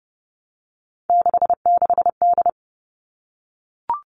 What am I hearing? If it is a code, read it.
Morse code, heard as a keyed tone, about 30 wpm, 700 Hz. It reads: 66D
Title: All Cut Numbers - 3-Digit 30wpm